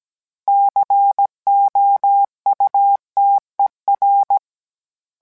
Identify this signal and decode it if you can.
Morse code, heard as a keyed tone, about 17 wpm, 800 Hz. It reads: COUTER